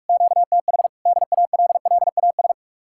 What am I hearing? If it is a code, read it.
Morse code, heard as a keyed tone, about 68 wpm, spaced as for 56 wpm, 700 Hz. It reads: QTH DALLAS
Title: Speed-Racing Common QSO Phrases - 45wpm